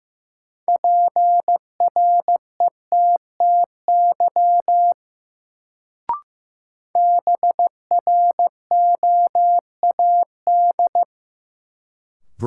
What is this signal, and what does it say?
Morse code, heard as a keyed tone, about 15 wpm, 700 Hz. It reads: PRETTY BROAD
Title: Sentences from Top 1000 Words - New Word Drill - 15wpm